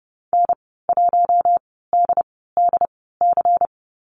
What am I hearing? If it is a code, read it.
Morse code, heard as a keyed tone, about 30 wpm, spaced as for 17 wpm, 700 Hz. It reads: N1DDC
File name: Lesson 19g - Letter w - Call Signs 30wpm